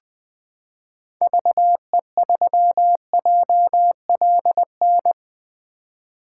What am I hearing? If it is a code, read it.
Morse code, heard as a keyed tone, about 20 wpm, 700 Hz. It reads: VE3JLN
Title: 2023 POTA Activator Call Signs - 20wpm